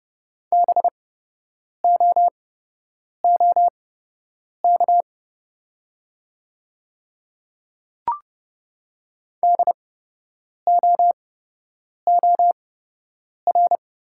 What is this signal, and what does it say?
Morse code, heard as a keyed tone, about 30 wpm, spaced as for 8 wpm, 700 Hz. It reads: BOOK DOOR
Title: 8x Character Spacing - Top 300 Words - 30wpm